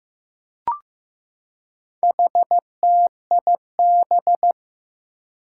Morse code, heard as a keyed tone, about 15 wpm, 700 Hz. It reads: HTIB
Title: Four Challenging Letter-Number Combinations 15wpm